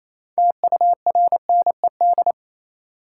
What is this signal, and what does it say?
Morse code, heard as a keyed tone, about 28 wpm, 700 Hz. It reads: TURNED
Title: Sets of 4 Words - New Word Drill - 28wpm